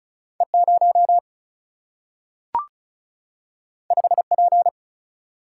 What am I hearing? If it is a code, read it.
Morse code, heard as a keyed tone, about 36 wpm, 700 Hz. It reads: E0 5P
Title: Single-Letter Single-Number 35wpm